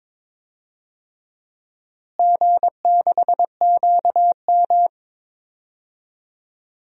Morse code, heard as a keyed tone, about 22 wpm, 700 Hz. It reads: G6QM
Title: DXCC-100 Call Signs - 4 Characters - Repeated 2x - 22wpm